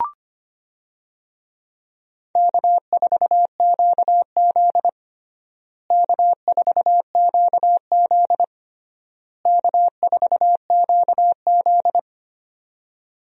Morse code, heard as a keyed tone, about 25 wpm, 700 Hz. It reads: K4QZ K4QZ K4QZ
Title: USA Call Signs - 4 Characters - Repeated 3x - 25wpm